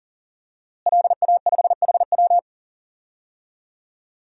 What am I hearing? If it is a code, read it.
Morse code, heard as a keyed tone, about 40 wpm, 700 Hz. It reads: LA5HW
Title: DXCC-100 Call Signs - 5 Characters - Repeated 3x - 40wpm